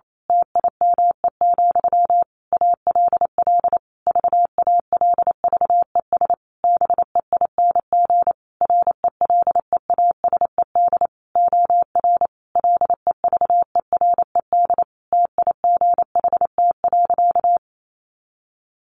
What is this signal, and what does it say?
Morse code, heard as a keyed tone, about 28 wpm, 700 Hz. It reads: TIME, ALL VALVES BEING RELEASED OR LEVERED TIGHT.
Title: A Sound Of Thunder 28wpm